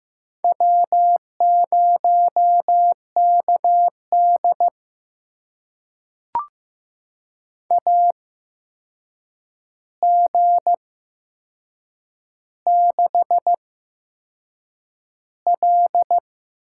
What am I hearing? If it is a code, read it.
Morse code, heard as a keyed tone, about 15 wpm, spaced as for 6 wpm, 700 Hz. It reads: W0KDA G6L
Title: USA Call Signs - 8x Character Spacing - 15wpm